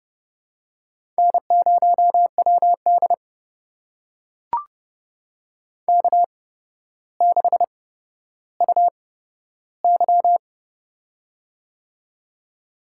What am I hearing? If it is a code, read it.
Morse code, heard as a keyed tone, about 31 wpm, spaced as for 13 wpm, 700 Hz. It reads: N0WD K6UY
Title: USA Call Signs - 4 Characters - 8x Character Spacing - 30wpm